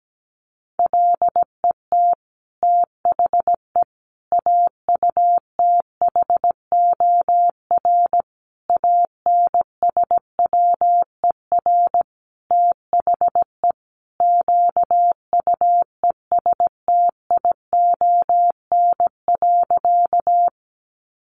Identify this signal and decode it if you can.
Morse code, heard as a keyed tone, about 17 wpm, 700 Hz. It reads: LET THE AUTHOR ANSWER THE QUESTION.
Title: Sentences from Top 1200 Words 17wpm